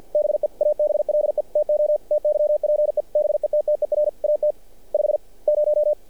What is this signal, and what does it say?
Morse code, heard as a keyed tone, about 48 wpm, 600 Hz. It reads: BE ABLE TO TYPE BETTER AT 50
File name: archived / CRSnet